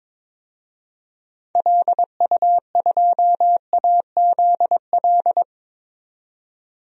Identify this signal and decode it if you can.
Morse code, heard as a keyed tone, about 21 wpm, 700 Hz. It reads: LU2AZL
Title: DXCC-100 Call Signs - 6 Characters - 22wpm